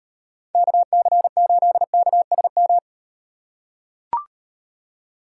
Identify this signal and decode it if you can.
Morse code, heard as a keyed tone, about 38 wpm, 700 Hz. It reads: KC8KSM